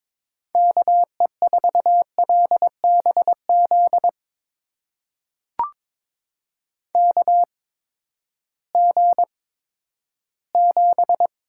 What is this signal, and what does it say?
Morse code, heard as a keyed tone, about 21 wpm, spaced as for 12 wpm, 700 Hz. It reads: KE4LBZ KG7
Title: USA Call Signs - 6 Characters - 8x Character Spacing - 22wpm